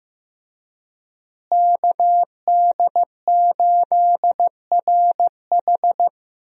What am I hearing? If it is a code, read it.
Morse code, heard as a keyed tone, about 15 wpm, 700 Hz. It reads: KD8RH
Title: USA Call Signs - 5 Characters - Repeated 2x - 15wpm